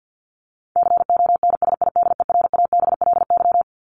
Morse code, heard as a keyed tone, about 50 wpm, 700 Hz. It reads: CONSIDERABLY
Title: LY Suffix - Top 100 Words with Base Words 50wpm